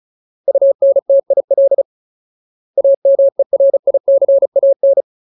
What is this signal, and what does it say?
Morse code, heard as a keyed tone, about 35 wpm, 550 Hz. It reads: UNTIL AMERICAN
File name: Ultimate Word List / 35WPM Ultimate